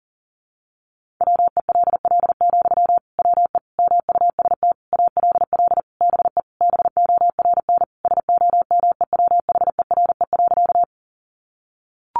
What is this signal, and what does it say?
Morse code, heard as a keyed tone, about 40 wpm, 700 Hz. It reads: WELL, WE MUST ALL BE BORN SOMEWHERE.
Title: Sentences from Top 1400 Words 40wpm